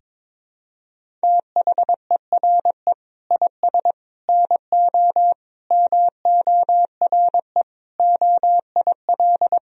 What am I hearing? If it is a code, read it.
Morse code, heard as a keyed tone, about 22 wpm, 700 Hz. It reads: THERE IS NO MORE OIL
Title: Sentences from Top 1000 Words - Review Part 4 - 22wpm